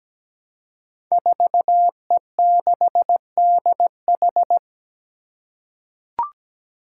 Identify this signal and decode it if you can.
Morse code, heard as a keyed tone, about 17 wpm, 700 Hz. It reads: 4E6DH
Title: Five Challenging Letter-Number Combinations 17wpm